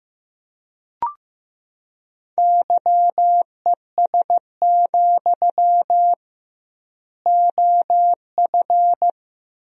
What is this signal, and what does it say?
Morse code, heard as a keyed tone, about 15 wpm, 700 Hz. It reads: YES, OF
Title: 2x Word-Spacing - Sentences from Top 500 Words - 15wpm